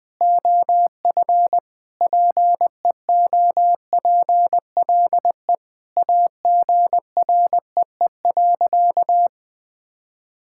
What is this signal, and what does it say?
Morse code, heard as a keyed tone, about 20 wpm, 700 Hz. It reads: OF PEOPLE AGREE.
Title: Sentences with 5-Word Phrases - Spoken Beginning - 20wpm